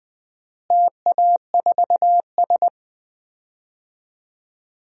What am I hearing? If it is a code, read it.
Morse code, heard as a keyed tone, about 20 wpm, 700 Hz. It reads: TA4S
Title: DXCC-100 Call Signs - 4 Characters - Repeated 3x - 20wpm